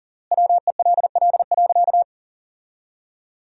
Morse code, heard as a keyed tone, about 40 wpm, 700 Hz. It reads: WELL.
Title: 3x Word-Spacing - Sentences from Top 300 Words - 40wpm